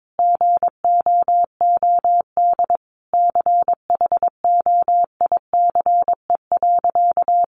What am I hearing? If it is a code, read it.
Morse code, heard as a keyed tone, about 22 wpm, 700 Hz. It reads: GOOD CHOICE.
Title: Sentences from Top 900 Words 22wpm